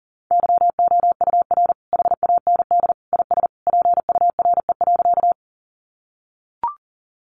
Morse code, heard as a keyed tone, about 40 wpm, 700 Hz. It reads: YOUR HAND IS PURE.
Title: Sentences from Top 1500 Words 40wpm